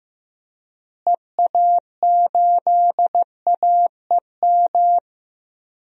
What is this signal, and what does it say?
Morse code, heard as a keyed tone, about 15 wpm, 700 Hz. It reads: EA8AEM
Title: DXCC-100 Call Signs - 6 Characters - Repeated 3x - 15wpm